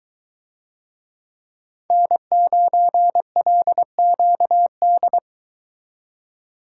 Morse code, heard as a keyed tone, about 21 wpm, 700 Hz. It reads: N9LQD